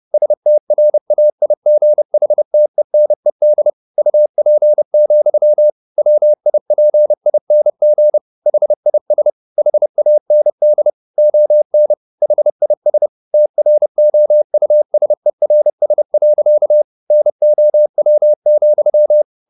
Morse code, heard as a keyed tone, about 30 wpm, 600 Hz. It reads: STRAIGHTENED UP, WIPING HIS HAND ON HIS TROUSERS. NOW,
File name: the old man and the sea / the old man and the sea 1 30wpm